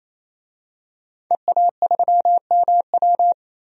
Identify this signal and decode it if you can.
Morse code, heard as a keyed tone, about 28 wpm, 700 Hz. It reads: EA3MW
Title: DXCC-100 Call Signs - 5 Characters - 28wpm